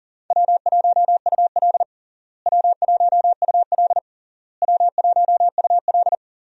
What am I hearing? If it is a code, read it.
Morse code, heard as a keyed tone, about 37 wpm, 700 Hz. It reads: W1UL W1UL W1UL
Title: USA Call Signs - 4 Characters - Repeated 3x - 40wpm